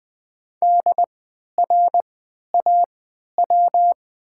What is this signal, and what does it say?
Morse code, heard as a keyed tone, about 20 wpm, spaced as for 11 wpm, 700 Hz. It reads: DRAW
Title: 3x Character Spacing - Top 500 Words - 20wpm